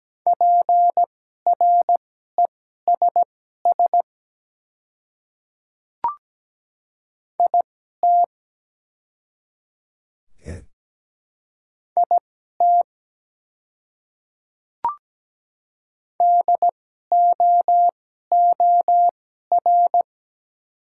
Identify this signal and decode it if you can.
Morse code, heard as a keyed tone, about 17 wpm, spaced as for 12 wpm, 700 Hz. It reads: PRESS IT IT DOOR
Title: 2x Character Spacing - Top 300 Words - 17wpm